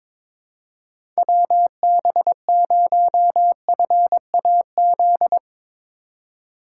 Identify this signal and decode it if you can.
Morse code, heard as a keyed tone, about 22 wpm, 700 Hz. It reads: WB0FAZ
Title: USA Call Signs - 6 Characters - 2x Character Spacing - 22wpm